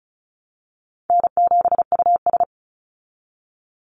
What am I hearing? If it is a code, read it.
Morse code, heard as a keyed tone, about 34 wpm, 700 Hz. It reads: N7US
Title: Call Signs Repeated 3x - 35wpm